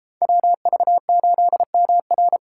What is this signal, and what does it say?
Morse code, heard as a keyed tone, about 33 wpm, 700 Hz. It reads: WV8MR